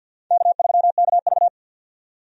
Morse code, heard as a keyed tone, about 47 wpm, 700 Hz. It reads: K3KV
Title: USA Call Signs - 4 Characters - 50wpm